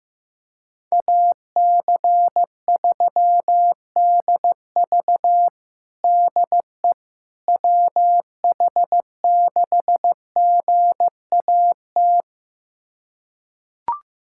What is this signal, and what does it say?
Morse code, heard as a keyed tone, about 16 wpm, 700 Hz. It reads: AC3DV DE WH6GAT
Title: USA Callsign DE Callsign 15wpm